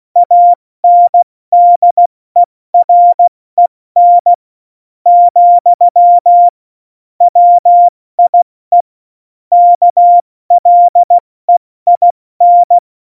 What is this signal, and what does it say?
Morse code, heard as a keyed tone, about 16 wpm, 700 Hz. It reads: ANDEREN , WIE KLEIN
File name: Morsekurs / Witze / Witz 10.Juli 2018, 70 F80